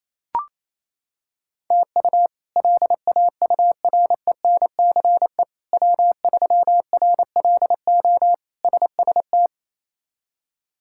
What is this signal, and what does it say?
Morse code, heard as a keyed tone, about 28 wpm, 700 Hz. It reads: TU LAURENCE W3RLO SST
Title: K1USN SST QSO Practice - 28wpm